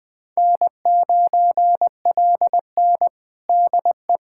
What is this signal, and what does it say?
Morse code, heard as a keyed tone, about 20 wpm, 700 Hz. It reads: N9LN DE
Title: USA Callsign DE Callsign - 4 Characters - 20wpm